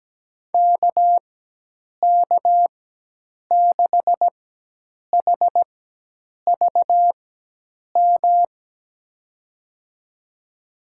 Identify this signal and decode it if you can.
Morse code, heard as a keyed tone, about 17 wpm, spaced as for 8 wpm, 700 Hz. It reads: KK6HVM
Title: USA Call Signs - 6 Characters - 4x Character Spacing - 17wpm